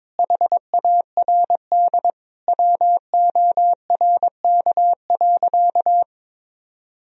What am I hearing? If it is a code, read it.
Morse code, heard as a keyed tone, about 22 wpm, 700 Hz. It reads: HARD WORK.
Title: Sentences from Top 1000 Words - Review Part 3 - 22wpm